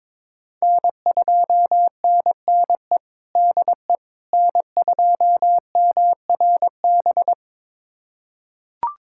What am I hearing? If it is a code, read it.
Morse code, heard as a keyed tone, about 22 wpm, 700 Hz. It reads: N2NNE DE N2MRB
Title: USA Callsign DE Callsign - 5 Characters - 22wpm